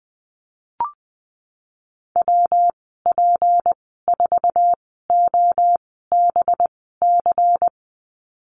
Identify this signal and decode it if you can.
Morse code, heard as a keyed tone, about 20 wpm, spaced as for 15 wpm, 700 Hz. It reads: WP4OBC
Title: 231 Lesson 37g - Letter x - Call Signs 20wpm